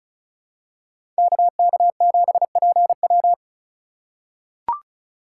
Morse code, heard as a keyed tone, about 36 wpm, 700 Hz. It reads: KK7PW
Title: USA Call Signs - 5 Characters - 4x Character Spacing - 35wpm